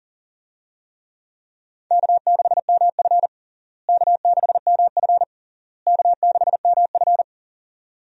Morse code, heard as a keyed tone, about 40 wpm, 700 Hz. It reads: K6MF K6MF K6MF
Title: USA Call Signs - 4 Characters - Repeated 3x - 40wpm